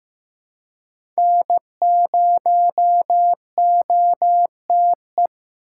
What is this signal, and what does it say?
Morse code, heard as a keyed tone, about 15 wpm, 700 Hz. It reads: N0OTE
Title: USA Call Signs - 8x Character Spacing - 15wpm